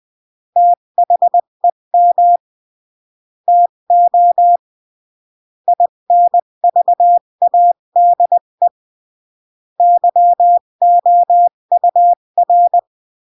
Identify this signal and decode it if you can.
Morse code, heard as a keyed tone, about 20 wpm, 700 Hz. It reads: THEM TO INVADE YOUR
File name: CW / jodorowsky / 82 commandments 20-15